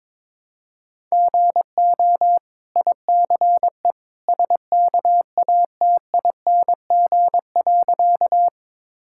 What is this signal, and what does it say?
Morse code, heard as a keyed tone, about 23 wpm, 690 Hz. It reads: GO ICE SKATING.
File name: practice / 22wpm / Easy Sentences with Hints - 22wpm